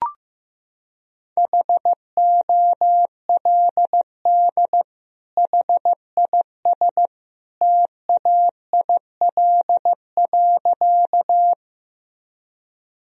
Morse code, heard as a keyed tone, about 15 wpm, 700 Hz. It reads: HOLD HIS TAIL.
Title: Sentences from Top 1000 Words - Review Part 3 - 15wpm